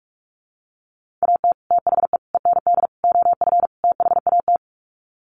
Morse code, heard as a keyed tone, about 44 wpm, 700 Hz. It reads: AT THE END OF THAT
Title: Sets of 5 Words Encore2 45wpm